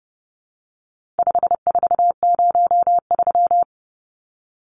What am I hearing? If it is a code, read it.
Morse code, heard as a keyed tone, about 30 wpm, 700 Hz. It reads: 5403
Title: Numbers - 4-Digit 30wpm